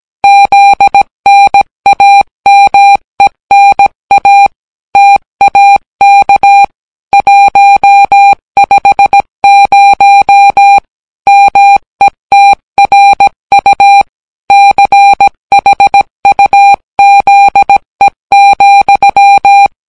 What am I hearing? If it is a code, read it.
Morse code, heard as a keyed tone, about 17 wpm, 800 Hz. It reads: ZNAMENA TAK 150 METRU CHUZE,